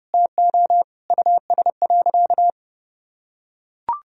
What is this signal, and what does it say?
Morse code, heard as a keyed tone, about 30 wpm, 700 Hz. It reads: TO US.
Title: Sentences from Top 1000 Words - Review Part 4 - 30wpm